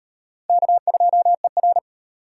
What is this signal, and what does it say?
Morse code, heard as a keyed tone, about 38 wpm, 700 Hz. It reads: K2ER